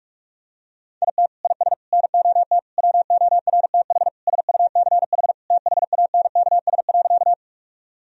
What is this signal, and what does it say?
Morse code, heard as a keyed tone, about 42 wpm, 700 Hz. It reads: IT IS NOT WORTH SUCH THANKS.
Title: Sentences from Top 1000 Words - Review Part 3 - 45wpm